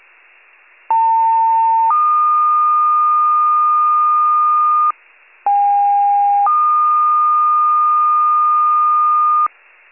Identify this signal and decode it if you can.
Two-tone page, heard as a one-second tone, then a three-second tone, about 900 then 1220 Hz, about 810 then 1220 Hz. It is heard from 0.9 to 9.5 s.